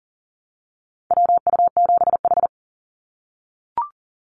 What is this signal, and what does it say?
Morse code, heard as a keyed tone, about 40 wpm, 700 Hz. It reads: WU7H
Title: NA SOTA Activator Callsigns 40wpm